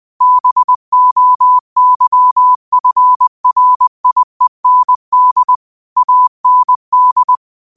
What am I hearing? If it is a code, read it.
Morse code, heard as a keyed tone, about 20 wpm, 1000 Hz. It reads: BOYFRIEND AND
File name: qotdfiles / 20wpm